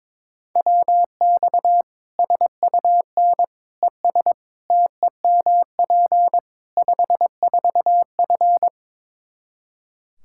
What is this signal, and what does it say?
Morse code, heard as a keyed tone, about 22 wpm, 700 Hz. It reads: WX SUN ES TEMP 54F
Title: Practice QSO - Exchange 2 - Info - 22wpm